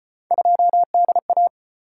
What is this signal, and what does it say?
Morse code, heard as a keyed tone, about 32 wpm, 700 Hz. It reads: 2DA